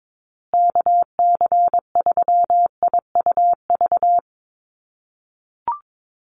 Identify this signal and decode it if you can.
Morse code, heard as a keyed tone, about 22 wpm, 700 Hz. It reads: KC3IUV
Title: USA Call Signs Repeated 2x 22wpm